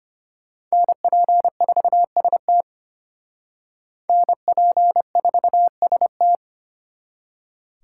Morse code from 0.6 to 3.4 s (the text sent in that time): NP4ST